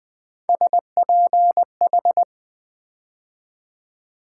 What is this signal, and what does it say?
Morse code, heard as a keyed tone, about 20 wpm, 700 Hz. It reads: SPH
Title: Build A Word - 3 Letters At A Time - 6-Letter Words - 20wpm